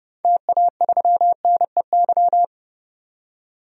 Morse code, heard as a keyed tone, about 30 wpm, 700 Hz. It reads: TA3NEY
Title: DXCC-100 Call Signs - 6 Characters - 30wpm